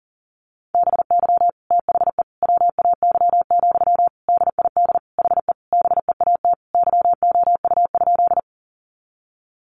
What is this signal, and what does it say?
Morse code, heard as a keyed tone, about 40 wpm, 700 Hz. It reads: BY THE WAY, DID HE BEAT YOU?
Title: Sentences from Top 1300 Words 40wpm